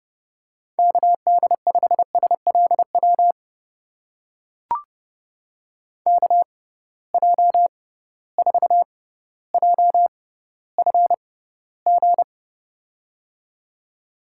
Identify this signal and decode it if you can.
Morse code, heard as a keyed tone, about 30 wpm, spaced as for 15 wpm, 700 Hz. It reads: KD5SLW KJ4JFG
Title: USA Call Signs - 6 Characters - 6x Character Spacing - 30wpm